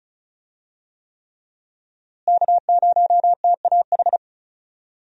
Morse code, heard as a keyed tone, about 36 wpm, 700 Hz. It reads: K0TAH
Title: USA Call Signs - 5 Characters - Repeated 3x - 35wpm